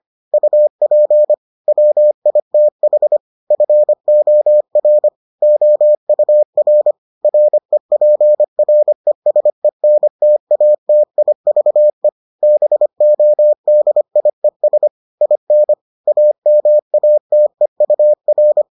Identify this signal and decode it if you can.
Morse code, heard as a keyed tone, about 25 wpm, 600 Hz. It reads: UP WITH FOR OUR REPRESENTATIVE BODIES IN AMATEUR